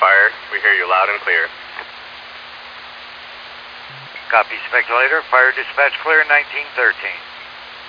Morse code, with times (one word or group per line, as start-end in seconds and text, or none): none